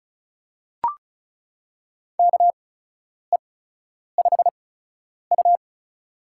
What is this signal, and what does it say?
Morse code, heard as a keyed tone, about 36 wpm, spaced as for 10 wpm, 700 Hz. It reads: KE5U
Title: USA Call Signs - 8x Character Spacing - 35wpm